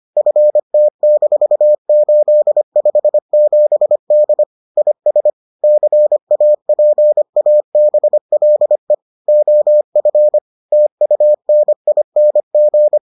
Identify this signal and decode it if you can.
Morse code, heard as a keyed tone, about 25 wpm, 600 Hz. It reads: FT-857D IS CAPABLE OF TUNING